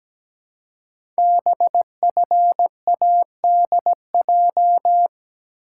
Morse code, heard as a keyed tone, about 17 wpm, 700 Hz. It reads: BFADJ
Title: Code Groups - 5 Letters - 17wpm